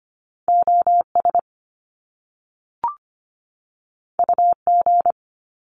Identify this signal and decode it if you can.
Morse code, heard as a keyed tone, about 25 wpm, 700 Hz. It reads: OS UG
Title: Top Two Letter Combinations in Words 25wpm